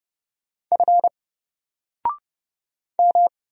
Morse code, heard as a keyed tone, about 30 wpm, 700 Hz. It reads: F M
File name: Single-Letter-Number-30wpm